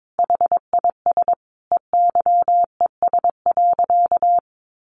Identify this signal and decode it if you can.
Morse code, heard as a keyed tone, about 22 wpm, 700 Hz. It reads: HIS EYES.
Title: Easy Phrases 22wpm